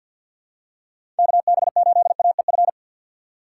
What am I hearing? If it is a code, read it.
Morse code, heard as a keyed tone, about 52 wpm, 700 Hz. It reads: KB8AEF
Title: USA Call Signs - 3x Character Spacing - 50wpm